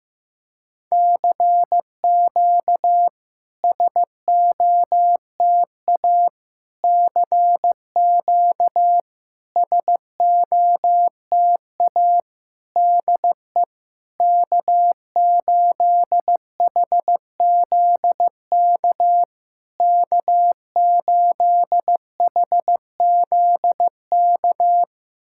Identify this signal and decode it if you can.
Morse code, heard as a keyed tone, about 15 wpm, 700 Hz. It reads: CQ SOTA CQ SOTA DE K8HZK K8HZK
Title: SOTA QSO Practice 15wpm